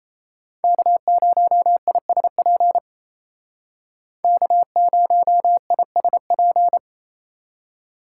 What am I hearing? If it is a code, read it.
Morse code, heard as a keyed tone, about 36 wpm, 700 Hz. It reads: K0ISP K0ISP